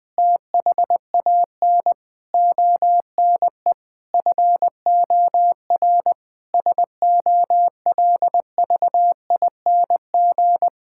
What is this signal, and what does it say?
Morse code, heard as a keyed tone, about 20 wpm, 700 Hz. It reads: THAN ONE FOR SOLVING
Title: Difficult Sentences with Hints - 20wpm